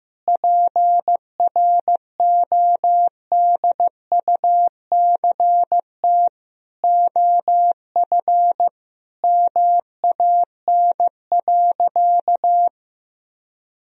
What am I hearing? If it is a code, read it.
Morse code, heard as a keyed tone, about 15 wpm, 700 Hz. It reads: PRODUCT OF MAN.
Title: Sentences from Top 1000 Words - Review Part 2 - 15wpm